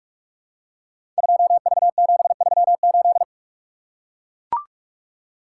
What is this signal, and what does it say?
Morse code, heard as a keyed tone, about 44 wpm, 700 Hz. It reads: 2V738
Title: Code Groups - 5 Letter:Numbers - 45wpm